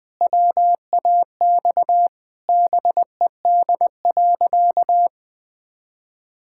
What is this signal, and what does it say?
Morse code, heard as a keyed tone, about 20 wpm, 700 Hz. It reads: WAX BED.
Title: Pangram Challenge v2 - 20wpm